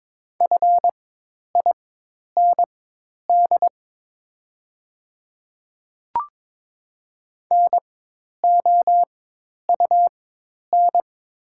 Morse code, heard as a keyed tone, about 22 wpm, spaced as for 10 wpm, 700 Hz. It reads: FIND NOUN
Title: 4x Character Spacing - Top 500 Words - 22wpm